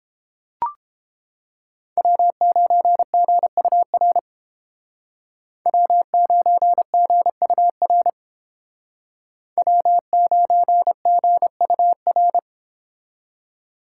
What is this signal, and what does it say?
Morse code, heard as a keyed tone, about 30 wpm, 700 Hz. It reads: W9GUR W9GUR W9GUR